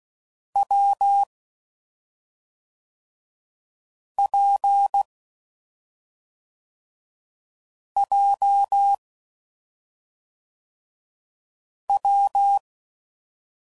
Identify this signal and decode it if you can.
Morse code, heard as a keyed tone, about 16 wpm, 800 Hz. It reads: W P J W
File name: Combinations / Difficult Combos / AWJ1